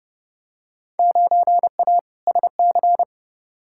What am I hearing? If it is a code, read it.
Morse code, heard as a keyed tone, about 30 wpm, 700 Hz. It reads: 9A SC
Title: Field Day Exchanges Encore - 30wpm